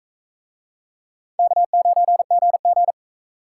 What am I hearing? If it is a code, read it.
Morse code, heard as a keyed tone, about 42 wpm, 700 Hz. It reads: K9GG